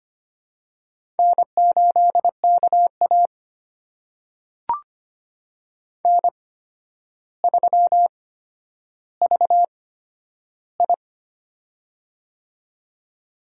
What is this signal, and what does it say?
Morse code, heard as a keyed tone, about 25 wpm, spaced as for 11 wpm, 700 Hz. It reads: N8KA N3VI
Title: USA Call Signs - 4 Characters - 8x Character Spacing - 25wpm